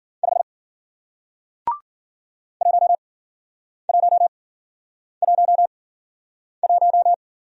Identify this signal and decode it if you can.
Morse code, heard as a keyed tone, about 55 wpm, 700 Hz. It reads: 5 1 1 1 1